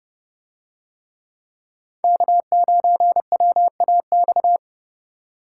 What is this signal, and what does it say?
Morse code, heard as a keyed tone, about 27 wpm, 700 Hz. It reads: K9WAX